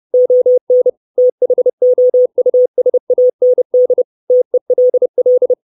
Morse code, heard as a keyed tone, about 30 wpm, 500 Hz. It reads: ON THOUSAND TELL
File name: 632 Words 30WPM